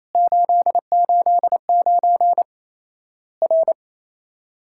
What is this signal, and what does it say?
Morse code, heard as a keyed tone, about 28 wpm, 700 Hz. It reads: 889 R
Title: Worked All Europe DX Contest - QTC Exchange - 28wpm